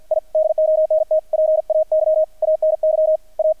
Morse code, heard as a keyed tone, about 48 wpm, 630 Hz, with radio noise. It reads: I DONT PAY ANY A
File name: archived / CRSnet